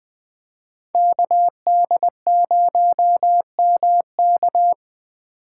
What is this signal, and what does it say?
Morse code, heard as a keyed tone, about 20 wpm, 700 Hz. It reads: KD0MK
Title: USA Call Signs - 5 Characters - 2x Character Spacing - 20wpm